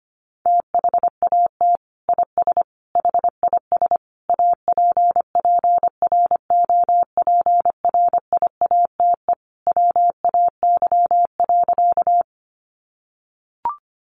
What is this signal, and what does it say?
Morse code, heard as a keyed tone, about 25 wpm, 700 Hz. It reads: THAT IS HIS APPROPRIATE WAY.
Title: Sentences from Top 900 Words 25wpm